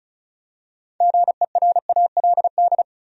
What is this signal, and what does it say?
Morse code, heard as a keyed tone, about 34 wpm, 700 Hz. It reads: GERALD
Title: USA Common Names - 35wpm